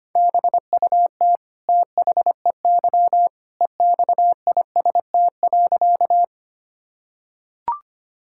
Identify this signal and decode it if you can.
Morse code, heard as a keyed tone, about 25 wpm, 700 Hz. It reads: BUT THEY EXIST.
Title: Sentences from Top 1000 Words - Review Part 5 - 25wpm